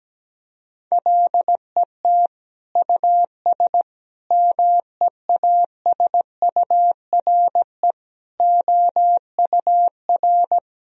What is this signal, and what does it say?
Morse code, heard as a keyed tone, about 17 wpm, 700 Hz. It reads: LET US MEASURE OUR
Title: Sentences from Top 1000 Words - Review Part 2 - 17wpm